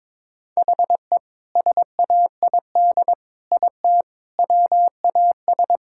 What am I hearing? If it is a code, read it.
Morse code, heard as a keyed tone, about 21 wpm, 700 Hz. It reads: HE SAID IT WAS
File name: practice / 22wpm / Sentences with 2-Word Phrases - 22wpm